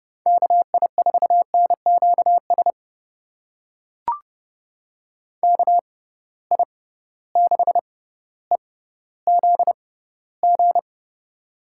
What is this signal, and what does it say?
Morse code, heard as a keyed tone, about 30 wpm, spaced as for 15 wpm, 700 Hz. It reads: KI4NQS KI6EZG
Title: USA Call Signs - 6 Characters - 6x Character Spacing - 30wpm